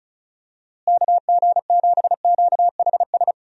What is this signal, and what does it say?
Morse code, heard as a keyed tone, about 36 wpm, 700 Hz. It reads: KG7QHS
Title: USA Call Signs - 6 Characters - Repeated 2x - 35wpm